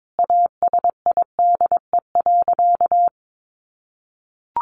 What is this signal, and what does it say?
Morse code, heard as a keyed tone, about 22 wpm, 700 Hz. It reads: ASIDE.
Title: A Sound Of Thunder 22wpm